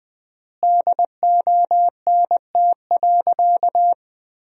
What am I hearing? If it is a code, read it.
Morse code, heard as a keyed tone, about 20 wpm, 700 Hz. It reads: DONT.
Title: 3x Word-Spacing - Sentences from Top 300 Words - 20wpm